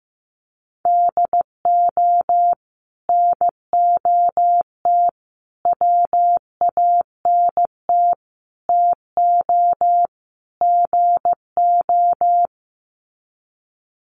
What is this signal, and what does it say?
Morse code, heard as a keyed tone, about 15 wpm, 700 Hz. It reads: DO NOT WANT TO GO
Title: Sets of 5 Words 15wpm